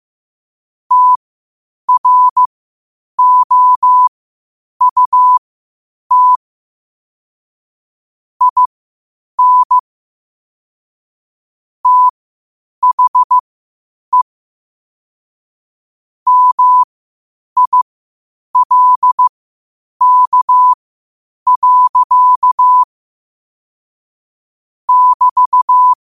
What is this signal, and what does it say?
Morse code, heard as a keyed tone, about 15 wpm, spaced as for 8 wpm, 1000 Hz. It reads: TROUT IN THE MILK. <BT>
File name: qotdfiles / 8wpm